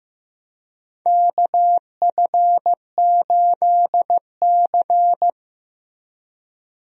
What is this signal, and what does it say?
Morse code, heard as a keyed tone, about 15 wpm, 700 Hz. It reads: KF8C
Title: USA Call Signs - 4 Characters - Repeated 3x - 15wpm